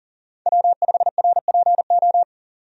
Morse code, heard as a keyed tone, about 37 wpm, 700 Hz. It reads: W5RPO
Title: USA Call Signs - 5 Characters - 40wpm